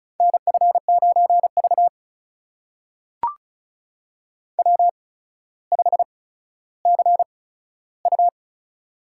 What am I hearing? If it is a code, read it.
Morse code, heard as a keyed tone, about 34 wpm, spaced as for 15 wpm, 700 Hz. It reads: NF9V W5CU
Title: USA Call Signs - 4 Characters - 8x Character Spacing - 35wpm